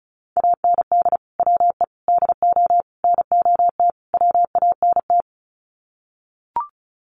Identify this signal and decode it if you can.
Morse code, heard as a keyed tone, about 34 wpm, 700 Hz. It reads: AND WE DO NOT WANT
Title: Sets of 5 Words Encore2 35wpm